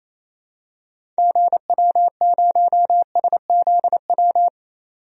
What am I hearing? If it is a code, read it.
Morse code, heard as a keyed tone, about 28 wpm, 700 Hz. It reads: GW0SZW
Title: DXCC-100 Call Signs - 6 Characters - 28wpm